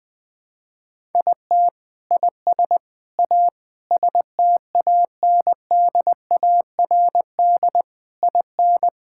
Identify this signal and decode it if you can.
Morse code, heard as a keyed tone, about 20 wpm, 700 Hz. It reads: IT IS A STANDARD IN
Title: Sentences from Top 1000 Words - Review Part 6 - 20wpm